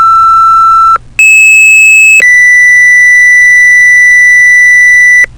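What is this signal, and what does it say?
Two-tone page, heard as a one-second tone, then a three-second tone, about 2600 then 2000 Hz, with noise. It is heard from 1.2 to 5.2 s.
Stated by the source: EMS call-out alert tone, used after the station tone.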